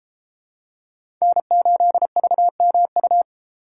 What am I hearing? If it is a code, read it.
Morse code, heard as a keyed tone, about 29 wpm, 700 Hz. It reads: N8VMU